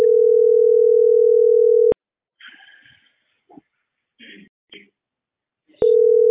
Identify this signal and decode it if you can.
Ringback tone (North America), 440 + 480 Hz.